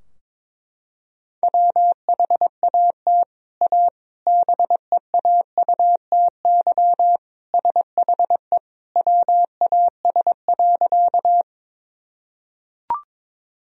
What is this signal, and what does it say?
Morse code, heard as a keyed tone, about 22 wpm, 700 Hz. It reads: WHAT A BEAUTY SHE WAS.
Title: Sentences from Top 1000 Words - Review Part 2 - 22wpm